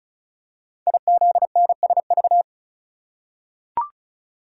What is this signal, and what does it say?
Morse code, heard as a keyed tone, about 36 wpm, 700 Hz. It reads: IZNSV
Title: Code Groups - 5 Letters - 35wpm